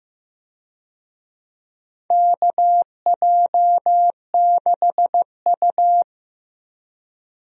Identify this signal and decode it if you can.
Morse code, heard as a keyed tone, about 15 wpm, 700 Hz. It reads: KJ6U
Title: USA Call Signs - 4 Characters - Repeated 2x - 15wpm